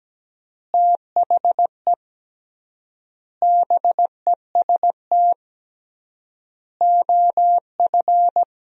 Morse code, heard as a keyed tone, about 17 wpm, 700 Hz. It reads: THE BEST OF
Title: 3x Word-Spacing - Sentences from Top 400 Words - 17wpm